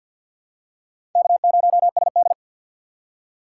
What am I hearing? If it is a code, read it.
Morse code, heard as a keyed tone, about 47 wpm, 700 Hz. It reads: K0SD
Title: USA Call Signs - 4 Characters - 6x Character Spacing - 50wpm